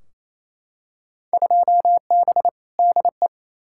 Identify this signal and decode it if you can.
Morse code, heard as a keyed tone, about 28 wpm, 700 Hz. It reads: 2B DE
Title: Field Day Exchanges Encore - 28wpm